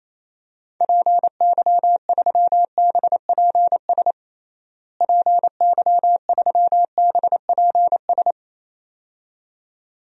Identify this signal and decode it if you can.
Morse code, heard as a keyed tone, about 28 wpm, 700 Hz. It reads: PY3BPS PY3BPS
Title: DXCC-100 Call Signs - 6 Characters - Repeated 2x - 28wpm